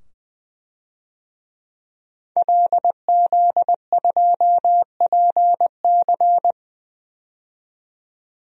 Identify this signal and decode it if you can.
Morse code, heard as a keyed tone, about 20 wpm, 700 Hz. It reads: LZ2PC
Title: DXCC-100 Call Signs - 5 Characters - 20wpm